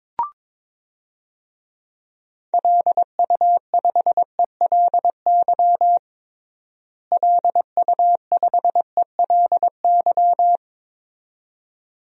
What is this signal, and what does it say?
Morse code, heard as a keyed tone, about 22 wpm, 700 Hz. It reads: LU5ELY LU5ELY
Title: DXCC-100 Call Signs - 6 Characters - Repeated 2x - 22wpm